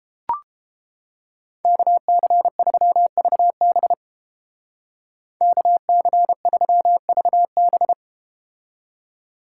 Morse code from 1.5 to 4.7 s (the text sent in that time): KC3VB